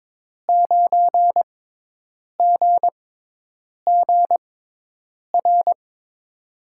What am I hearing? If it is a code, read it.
Morse code, heard as a keyed tone, about 22 wpm, spaced as for 8 wpm, 700 Hz. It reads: 9GGR
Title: USA Call Signs - 5 Characters - 6x Character Spacing - 22wpm